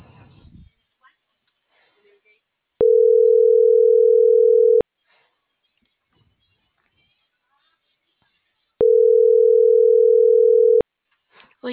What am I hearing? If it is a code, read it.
Ringback tone (North America), 440 + 480 Hz.